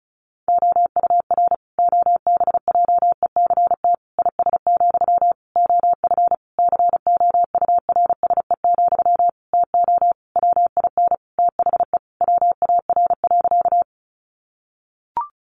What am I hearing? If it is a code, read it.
Morse code, heard as a keyed tone, about 34 wpm, 700 Hz. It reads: OUR OBJECT IS, OF COURSE, TO WIN THE WAR.
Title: Sentences from Top 1500 Words 35wpm